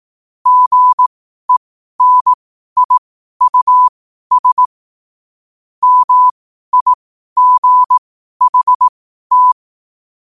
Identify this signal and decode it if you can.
Morse code, heard as a keyed tone, about 18 wpm, spaced as for 12 wpm, 1000 Hz. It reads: GENIUS MIGHT
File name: qotdfiles / 12wpm